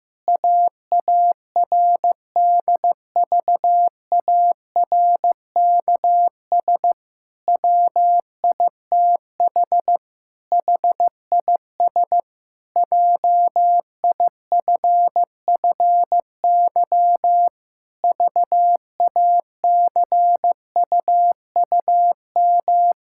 Morse code, heard as a keyed tone, about 15 wpm, 700 Hz. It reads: AARDVARKS WITH HIS JIFFY VACUUM
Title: Pangram Challenge v2 - 15wpm